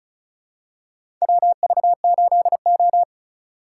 Morse code, heard as a keyed tone, about 36 wpm, 700 Hz. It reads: WV8O
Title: USA Call Signs - 4 Characters - 2x Character Spacing - 35wpm